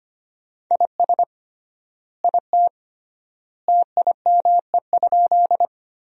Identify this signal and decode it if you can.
Morse code, heard as a keyed tone, about 25 wpm, 700 Hz. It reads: IS IT TIME?
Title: 3x Word-Spacing - Sentences from Top 100 Words - 25wpm